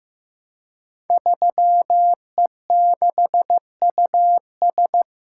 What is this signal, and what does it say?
Morse code, heard as a keyed tone, about 15 wpm, 700 Hz. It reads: 3E6US
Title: Code Groups - 5 Letter:Numbers - 15wpm